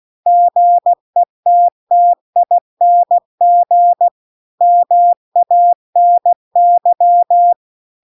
Morse code, heard as a keyed tone, about 16 wpm, 700 Hz. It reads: GETTING MANY